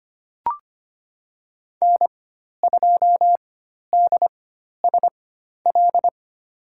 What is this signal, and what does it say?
Morse code, heard as a keyed tone, about 25 wpm, spaced as for 12 wpm, 690 Hz. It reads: N2DSL
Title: USA Call Signs - 4x Character Spacing - 25wpm